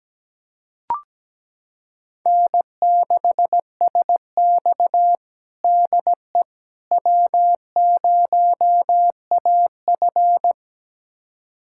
Morse code, heard as a keyed tone, about 17 wpm, 700 Hz. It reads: N6SX DE W0AF
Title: USA Callsign DE Callsign - 4 Characters - 17wpm